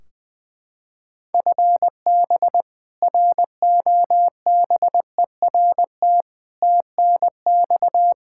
Morse code, heard as a keyed tone, about 20 wpm, 700 Hz. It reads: FB ROBERT TNX
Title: K1USN SST QSO Practice 20wpm